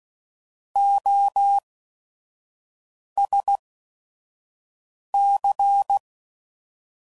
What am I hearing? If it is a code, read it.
Morse code, heard as a keyed tone, about 16 wpm, spaced as for 5 wpm, 800 Hz. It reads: OSC
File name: Suffixes / Special ING's / VerySlowToFaster ING's